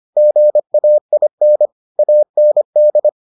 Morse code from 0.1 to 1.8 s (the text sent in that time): GAIN